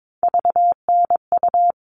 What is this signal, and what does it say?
Morse code, heard as a keyed tone, about 22 wpm, 700 Hz. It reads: VNU
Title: All Cut Numbers - 3-Digit 22wpm